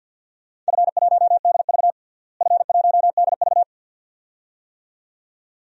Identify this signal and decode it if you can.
Morse code, heard as a keyed tone, about 52 wpm, 700 Hz. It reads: F1DV F1DV
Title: DXCC-100 Call Signs - 4 Characters - Repeated 2x - 50wpm